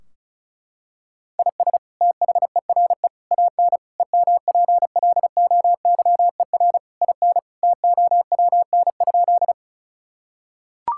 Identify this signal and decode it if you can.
Morse code, heard as a keyed tone, about 36 wpm, 700 Hz. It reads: IS THERE AN EMPLOYER IN TOWN?
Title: Sentences from Top 1600 Words 35wpm